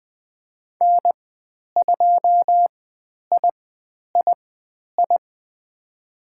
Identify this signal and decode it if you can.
Morse code, heard as a keyed tone, about 20 wpm, spaced as for 10 wpm, 700 Hz. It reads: N2III
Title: 102 Lesson 17g - Letter u - Call Signs 20wpm